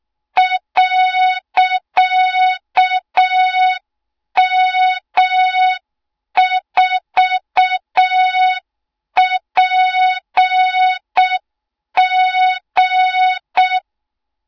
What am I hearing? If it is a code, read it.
Morse code, heard as a keyed tone, about 6 wpm, 3700 Hz. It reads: .M4PG
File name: Morze-HA2MN / hang / 20 GYAKORLAT